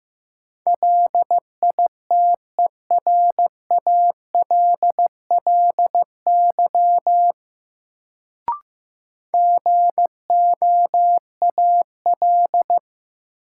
Morse code, heard as a keyed tone, about 15 wpm, 700 Hz. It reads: LITERALLY GOAL
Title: Sentences from Top 1400 Words - New Word Drill Alt - 15wpm